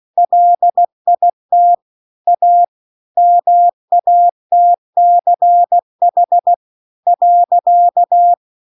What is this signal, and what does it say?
Morse code, heard as a keyed tone, about 16 wpm, 700 Hz. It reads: LIT A MATCH .